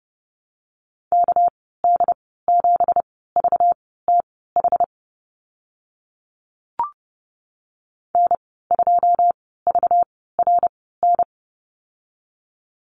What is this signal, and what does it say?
Morse code, heard as a keyed tone, about 30 wpm, spaced as for 17 wpm, 700 Hz. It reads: KD7VTH N2VRN
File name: Lesson 37g - Letter x - Call Signs 30wpm